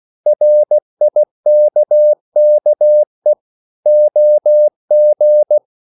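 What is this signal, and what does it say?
Morse code, heard as a keyed tone, about 16 wpm, 600 Hz. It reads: RIKKE OG
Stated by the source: Slow (QRS) Morse code at 16 wpm, from CW med Gnister.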